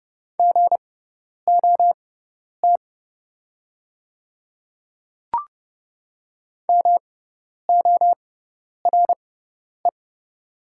Morse code, heard as a keyed tone, about 30 wpm, spaced as for 10 wpm, 700 Hz. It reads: GOT MORE
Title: 6x Character Spacing - Top 300 Words - 30wpm